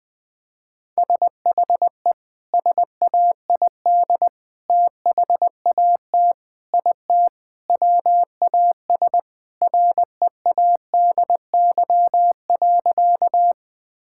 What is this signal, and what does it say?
Morse code, heard as a keyed tone, about 20 wpm, 700 Hz. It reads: SHE SAID THAT IT WAS READY.
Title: Sentences with 3-Word Phrases - 20wpm